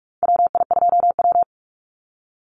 Morse code, heard as a keyed tone, about 44 wpm, 690 Hz. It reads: WI2W
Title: NA SOTA Chaser Callsigns 45wpm